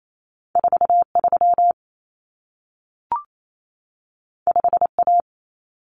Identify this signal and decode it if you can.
Morse code, heard as a keyed tone, about 28 wpm, 700 Hz. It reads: 43 5A
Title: Common Cut Numbers - 2-Digit 28wpm